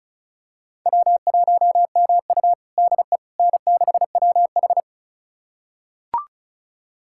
Morse code, heard as a keyed tone, about 36 wpm, 700 Hz. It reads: W1MU DE N6WH
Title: USA Callsign DE Callsign - 4 Characters - 35wpm